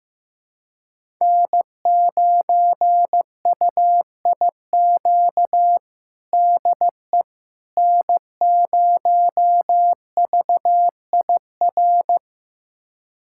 Morse code, heard as a keyed tone, about 15 wpm, 700 Hz. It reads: N9UIQ DE N0VIR
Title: USA Callsign DE Callsign - 5 Characters - 15wpm